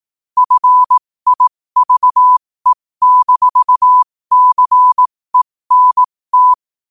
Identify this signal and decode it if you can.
Morse code, heard as a keyed tone, about 18 wpm, spaced as for 15 wpm, 1000 Hz. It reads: FIVE-CENT
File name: qotdfiles / 15wpm